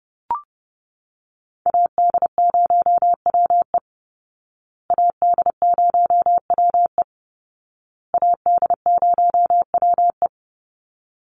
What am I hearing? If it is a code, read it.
Morse code, heard as a keyed tone, about 30 wpm, 700 Hz. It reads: AD0WE AD0WE AD0WE
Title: Call Signs Repeated 2x - 30wpm